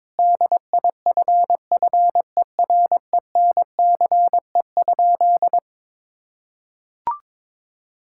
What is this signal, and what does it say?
Morse code, heard as a keyed tone, about 22 wpm, 700 Hz. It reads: DIFFERENCE?
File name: practice / 22wpm / Sentences with 2-Word Phrases - 22wpm